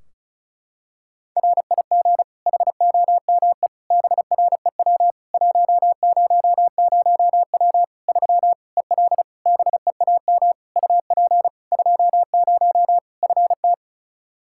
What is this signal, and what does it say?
Morse code, heard as a keyed tone, about 36 wpm, 700 Hz. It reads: RIG HOME BREW 100W 3 EL BEAM UP 20 FT
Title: Practice QSO - Exchange 2 - Info - 35wpm